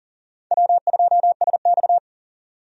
Morse code, heard as a keyed tone, about 40 wpm, 700 Hz. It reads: W2SX
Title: USA Call Signs - 4 Characters - 40wpm